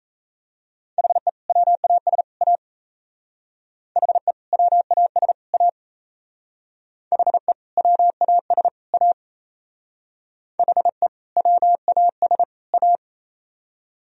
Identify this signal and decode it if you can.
Morse code, heard as a keyed tone, about 37 wpm, 700 Hz. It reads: HE WAS A HE WAS A HE WAS A HE WAS A